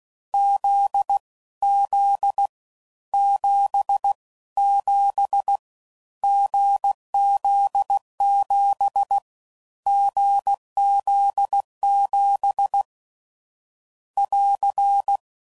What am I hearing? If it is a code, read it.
Morse code, heard as a keyed tone, about 16 wpm, 800 Hz. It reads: Z Z 7 7 GZ7 GZ7 <AR>
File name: Combinations / Difficult Combos / awj1 ndb6 gz7